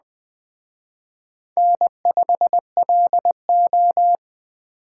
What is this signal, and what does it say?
Morse code, heard as a keyed tone, about 20 wpm, 700 Hz. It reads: N5LO